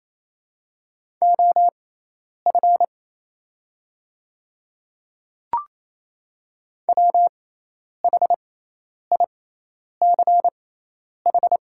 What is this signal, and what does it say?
Morse code, heard as a keyed tone, about 28 wpm, spaced as for 10 wpm, 690 Hz. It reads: OF WHICH
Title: 6x Character Spacing - Top 500 Words - 28wpm